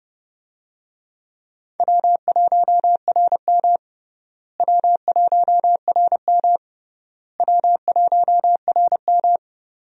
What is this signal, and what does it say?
Morse code, heard as a keyed tone, about 32 wpm, 700 Hz. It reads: W1RM W1RM W1RM
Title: USA Call Signs - 4 Characters - Repeated 3x - 30wpm